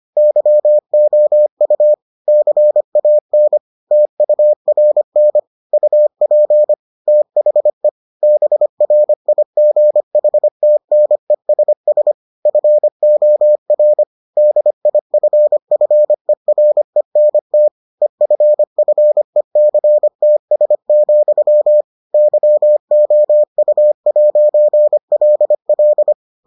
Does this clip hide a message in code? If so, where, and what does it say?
Yes: Morse code from 0.2 to 26.1 s, heard as a keyed tone, about 25 wpm, 600 Hz. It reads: YOU CAN TURN UP THE BRIGHTNESS FOR DIFFERENT EFFECTS, YOU'LL